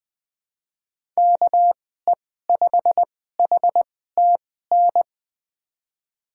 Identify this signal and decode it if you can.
Morse code, heard as a keyed tone, about 20 wpm, spaced as for 14 wpm, 700 Hz. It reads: KE5HTN
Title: USA Call Signs - 6 Characters - 2x Character Spacing - 20wpm